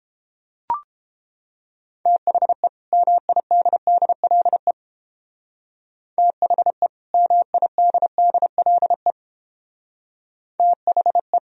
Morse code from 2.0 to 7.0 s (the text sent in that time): THE MIDDLE THE